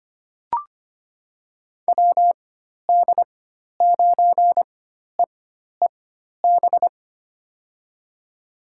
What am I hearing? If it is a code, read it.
Morse code, heard as a keyed tone, about 25 wpm, spaced as for 12 wpm, 700 Hz. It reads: WD9EEB
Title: USA Call Signs - 6 Characters - 4x Character Spacing - 25wpm